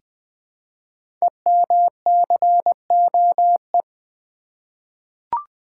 Morse code, heard as a keyed tone, about 20 wpm, 700 Hz. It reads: EMCOE
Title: Code Groups - 5 Letter:Numbers - 20wpm